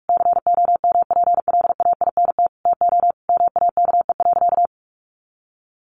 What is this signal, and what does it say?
Morse code, heard as a keyed tone, about 44 wpm, 700 Hz. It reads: COMPLAINT TO MAKE.
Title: Sentences from Top 1400 Words 45wpm